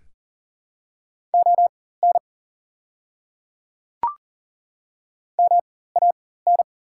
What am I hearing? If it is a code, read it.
Morse code, heard as a keyed tone, about 40 wpm, spaced as for 19 wpm, 700 Hz. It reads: ON MAN
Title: 4x Character Spacing - Top 200 Words - 40wpm